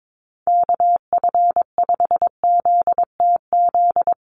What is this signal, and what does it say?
Morse code, heard as a keyed tone, about 22 wpm, 700 Hz. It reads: KF5ZTZ
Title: USA Call Signs Repeated 3x 22wpm